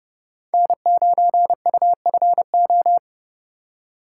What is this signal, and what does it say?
Morse code, heard as a keyed tone, about 39 wpm, 700 Hz. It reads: N9UFO